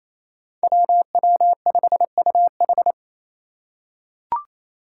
Morse code, heard as a keyed tone, about 27 wpm, 700 Hz. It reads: WW5UH
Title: Five Challenging Letter-Number Combinations 28wpm